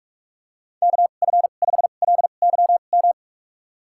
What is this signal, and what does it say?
Morse code, heard as a keyed tone, about 42 wpm, spaced as for 31 wpm, 700 Hz. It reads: KF5LYM
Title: USA Call Signs - 6 Characters - 2x Character Spacing - 45wpm